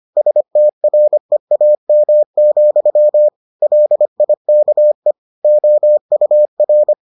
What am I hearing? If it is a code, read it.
Morse code, heard as a keyed tone, about 25 wpm, 600 Hz. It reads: STREAM, LIKE OUR